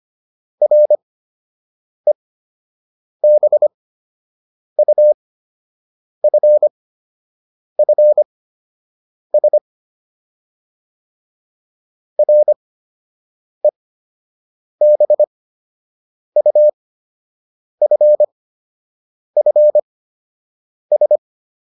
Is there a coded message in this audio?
Yes: Morse code from 0.6 to 21.2 s, heard as a keyed tone, about 25 wpm, spaced as for 7 wpm, 600 Hz. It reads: REBUFFS REBUFFS